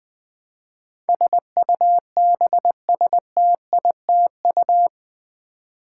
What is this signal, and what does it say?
Morse code, heard as a keyed tone, about 20 wpm, 700 Hz. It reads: SUBSTITU
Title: Build A Word - 4 Letters At A Time - 12-Letter Words - 20wpm